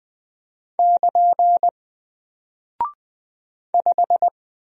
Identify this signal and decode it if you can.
Morse code, heard as a keyed tone, about 20 wpm, 700 Hz. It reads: <KN> 5
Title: Single Character (Letter:Number:Punctuation:Prosign) 20wpm